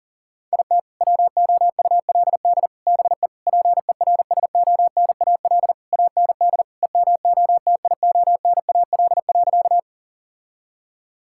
Morse code, heard as a keyed tone, about 40 wpm, 700 Hz. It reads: IT WOULD BE PERSONAL AND EMOTIONAL.
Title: Sentences from Top 1000 Words - Review Part 5 - 40wpm